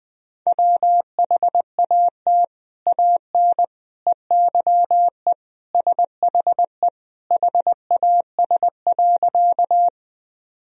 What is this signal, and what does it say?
Morse code, heard as a keyed tone, about 20 wpm, 700 Hz. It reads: WHAT AN EYE SHE HAS.
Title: Sentences from Top 1000 Words - Review Part 1 - 20wpm